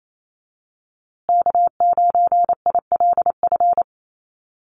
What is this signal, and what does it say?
Morse code, heard as a keyed tone, about 28 wpm, 700 Hz. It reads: K9ILF
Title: USA Call Signs Repeated 3x 28wpm